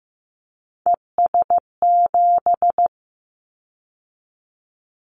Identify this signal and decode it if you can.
Morse code, heard as a keyed tone, about 15 wpm, 700 Hz. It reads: ES7
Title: Callsign Prefixes for Top 100 DXCC - 15wpm